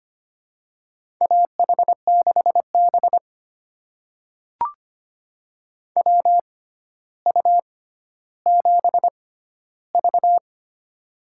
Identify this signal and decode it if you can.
Morse code, heard as a keyed tone, about 25 wpm, spaced as for 13 wpm, 700 Hz. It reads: AH6B WU7V
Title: USA Call Signs - 4 Characters - 6x Character Spacing - 25wpm